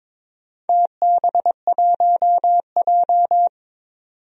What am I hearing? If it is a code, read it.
Morse code, heard as a keyed tone, about 22 wpm, 700 Hz. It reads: TB1J
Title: DXCC-100 Call Signs - 4 Characters - 22wpm